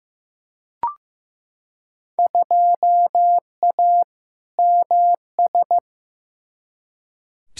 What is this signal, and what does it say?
Morse code, heard as a keyed tone, about 15 wpm, 700 Hz. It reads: 2A MS
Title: Field Day Exchanges - 15wpm